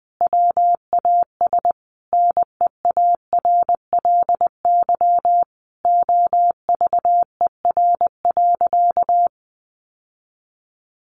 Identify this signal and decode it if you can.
Morse code, heard as a keyed tone, about 20 wpm, 700 Hz. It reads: WAS NEARLY OVER.
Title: Sentences from Top 1500 Words 20wpm